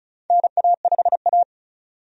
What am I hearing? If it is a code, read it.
Morse code, heard as a keyed tone, about 36 wpm, 700 Hz. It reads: NA5A
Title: USA Call Signs - 4 Characters - 35wpm